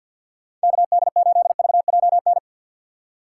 Morse code, heard as a keyed tone, about 52 wpm, 700 Hz. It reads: KD8VJN
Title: USA Call Signs - 6 Characters - 4x Character Spacing - 50wpm